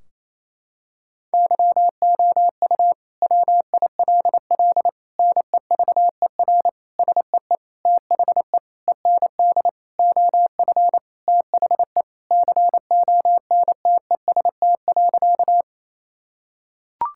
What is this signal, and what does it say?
Morse code, heard as a keyed tone, about 27 wpm, 700 Hz. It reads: YOU WILL NEVER SEE THE END OF THE CONTEST.
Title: Sentences from Top 1600 Words 28wpm